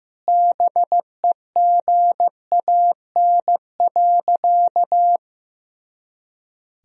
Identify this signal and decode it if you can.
Morse code, heard as a keyed tone, about 15 wpm, 700 Hz. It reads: BEGAN.
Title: Sentences from Top 400 Words and Abbreviations 15wpm